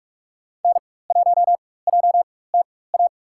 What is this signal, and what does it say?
Morse code, heard as a keyed tone, about 44 wpm, spaced as for 21 wpm, 700 Hz. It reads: N1JTA
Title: USA Call Signs - 5 Characters - 4x Character Spacing - 45wpm